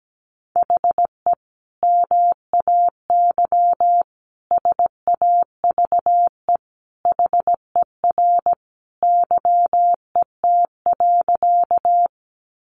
Morse code, heard as a keyed tone, about 17 wpm, 700 Hz. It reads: HE MAY SAVE HER YET.
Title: Sentences from Top 1100 Words 17wpm